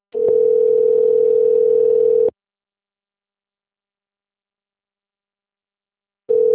Ringback tone (North America), 440 + 480 Hz.